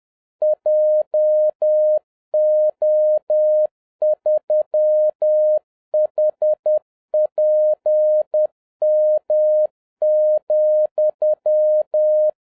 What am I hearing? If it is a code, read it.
Morse code, heard as a keyed tone, about 10 wpm, 610 Hz. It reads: JO3HPM,
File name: 2011-07-mp3 / n36 THE 3.11 2 10wpm